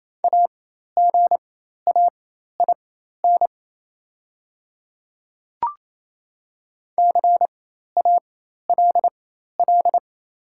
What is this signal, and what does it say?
Morse code, heard as a keyed tone, about 27 wpm, spaced as for 13 wpm, 700 Hz. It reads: AGAIN CALL
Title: 4x Character Spacing - Top 500 Words - 28wpm